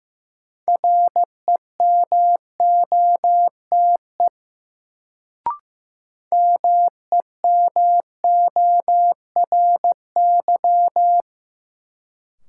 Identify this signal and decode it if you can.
Morse code, heard as a keyed tone, about 15 wpm, 700 Hz. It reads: REMOTE MEMORY
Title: Sentences from Top 1300 Words - New Word Drill Alt - 15wpm